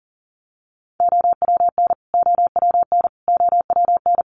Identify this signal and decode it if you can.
Morse code, heard as a keyed tone, about 40 wpm, 700 Hz. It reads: OWN OWN OWN
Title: Top 300 Words Repeated 40wpm